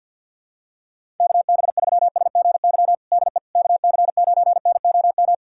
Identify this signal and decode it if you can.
Morse code, heard as a keyed tone, about 52 wpm, 700 Hz. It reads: KB3SGY DE KC9NOM
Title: USA Callsign DE Callsign - 6 Characters - 50wpm